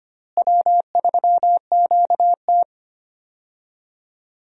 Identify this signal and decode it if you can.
Morse code, heard as a keyed tone, about 25 wpm, 700 Hz. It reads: W3QT
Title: USA Call Signs - 4 Characters - Repeated 2x - 25wpm